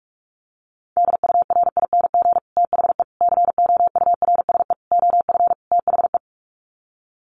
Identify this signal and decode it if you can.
Morse code, heard as a keyed tone, about 44 wpm, 700 Hz. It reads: DURING THE COURSE OF THE
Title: Sets of 5 Words Encore 45wpm